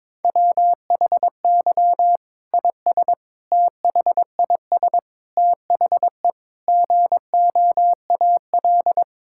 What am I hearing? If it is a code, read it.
Morse code, heard as a keyed tone, about 22 wpm, 700 Hz. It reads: WHY IS THIS THE GOAL
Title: Sentences from Top 1000 Words - Review Part 5 - 22wpm